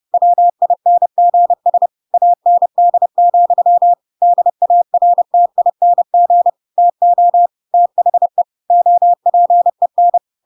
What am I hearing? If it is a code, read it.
Morse code, heard as a keyed tone, about 30 wpm, 700 Hz. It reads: WINGS AND, DARTING TO THE OPEN
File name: Pinocchio / Pinocchio ch5 30 30 ik1hge